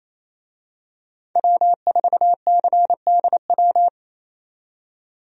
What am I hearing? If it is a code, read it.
Morse code, heard as a keyed tone, about 28 wpm, 700 Hz. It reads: W4CDW
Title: USA Call Signs - 5 Characters - 28wpm